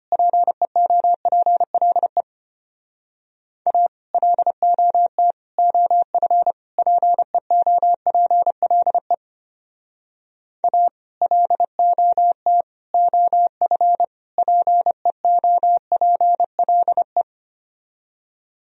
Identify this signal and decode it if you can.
Morse code, heard as a keyed tone, about 33 wpm, 700 Hz. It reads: PEOPLE A LOT OF PEOPLE A LOT OF PEOPLE